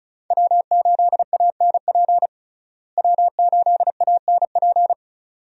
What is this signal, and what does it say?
Morse code, heard as a keyed tone, about 36 wpm, 700 Hz. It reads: W8ANP W8ANP
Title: USA Call Signs - 5 Characters - Repeated 2x - 35wpm